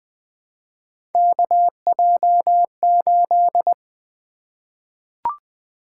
Morse code, heard as a keyed tone, about 20 wpm, 700 Hz. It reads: KJ8
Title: Three Letter-Number Combinations - 20wpm